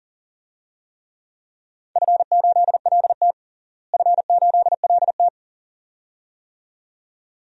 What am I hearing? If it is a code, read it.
Morse code, heard as a keyed tone, about 38 wpm, 700 Hz. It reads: F8LT F8LT
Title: DXCC-100 Call Signs - 4 Characters - Repeated 2x - 40wpm